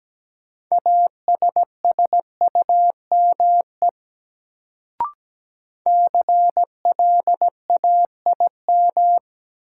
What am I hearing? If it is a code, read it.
Morse code, heard as a keyed tone, about 17 wpm, 700 Hz. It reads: ASSUME CLAIM
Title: Sentences from Top 1500 Words - New Word Drill - 17wpm